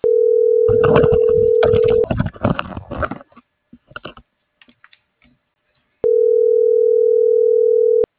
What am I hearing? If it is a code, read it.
Ringback tone (North America), 440 + 480 Hz.